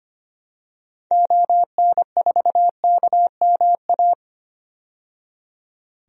Morse code, heard as a keyed tone, about 25 wpm, 700 Hz. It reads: ON4KMA
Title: DXCC-100 Call Signs - 6 Characters - 25wpm